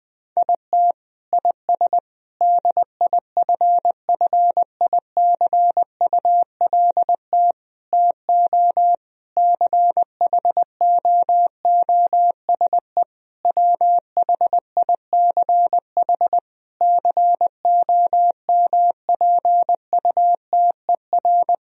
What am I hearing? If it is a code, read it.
Morse code, heard as a keyed tone, about 20 wpm, 700 Hz. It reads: IT IS DIFFICULT TO CHOOSE WHICH COMPUTER
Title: Sentences from Top 1000 Words - Review Part 3 - 20wpm